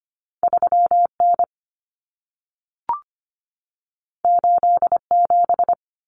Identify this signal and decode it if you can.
Morse code, heard as a keyed tone, about 25 wpm, 700 Hz. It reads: 3N 87
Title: Common Cut Numbers - 2-Digit 25wpm